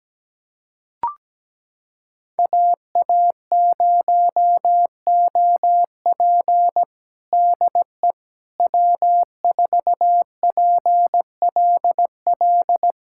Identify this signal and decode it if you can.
Morse code, heard as a keyed tone, about 17 wpm, 700 Hz. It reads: AA0OP DE W4PLL
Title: USA Callsign DE Callsign - 5 Characters - 17wpm